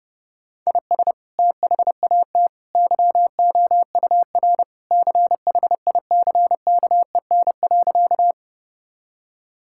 Morse code, heard as a keyed tone, about 30 wpm, 700 Hz. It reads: IS THAT YOUR CHICKEN.
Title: Sentences from Top 1000 Words - Review Part 5 - 30wpm